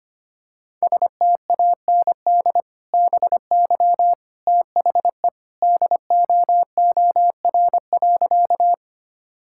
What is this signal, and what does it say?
Morse code, heard as a keyed tone, about 25 wpm, 700 Hz. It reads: STAND BY THE DOOR.
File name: practice / 25wpm / Sentences with 2-Word Phrases - 25wpm